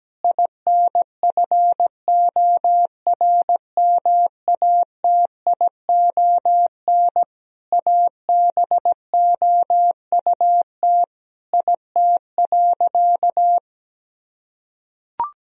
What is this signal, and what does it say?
Morse code, heard as a keyed tone, about 17 wpm, 700 Hz. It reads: INFORMATION ABOUT IT.
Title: Sentences from Top 1000 Words - Review Part 3 - 17wpm